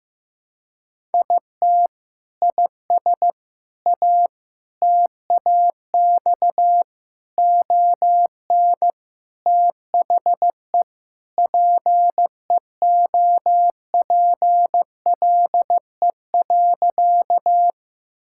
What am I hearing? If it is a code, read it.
Morse code, heard as a keyed tone, about 16 wpm, 700 Hz. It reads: IT IS A TAX ON THE PEOPLE.
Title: Sentences from Top 1000 Words - Review Part 5 - 15wpm